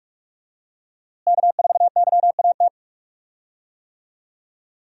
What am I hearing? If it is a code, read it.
Morse code, heard as a keyed tone, about 47 wpm, 700 Hz. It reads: K4YAT
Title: USA Call Signs - 5 Characters - Repeated 2x - 45wpm